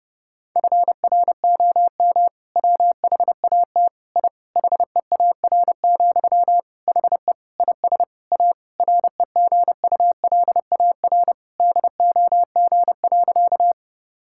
Morse code, heard as a keyed tone, about 32 wpm, 700 Hz. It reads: FROM WHAT I HEAR, HE IS A REGULAR DOG.
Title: Sentences from Top 1000 Words - Review Part 6 - 30wpm